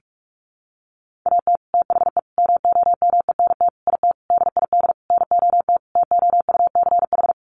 Morse code, heard as a keyed tone, about 44 wpm, 700 Hz. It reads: AT THE MOMENT IT DID NOT TOUCH
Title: Sentences from Top 1200 Words 45wpm